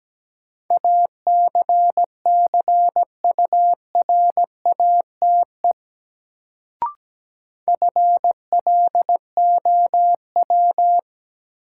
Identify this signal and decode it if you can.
Morse code, heard as a keyed tone, about 17 wpm, 700 Hz. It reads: ACCURATE FLOW
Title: Sentences from Top 1200 Words - New Word Drill - 17wpm